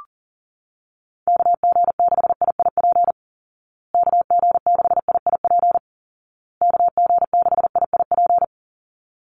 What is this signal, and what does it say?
Morse code, heard as a keyed tone, about 40 wpm, 700 Hz. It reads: KG6IIP KG6IIP KG6IIP
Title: Call Signs Repeated 2x - 40wpm